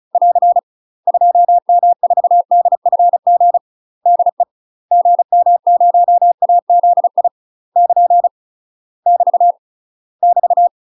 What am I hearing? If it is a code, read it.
Morse code, heard as a keyed tone, about 34 wpm, 700 Hz. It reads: <AR> 2M4DFG DE GM0AZI <KN> <BT> <BT>
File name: UK MASTER QSOs TRAINING / 35WPM / UK MASTER QSOs 0025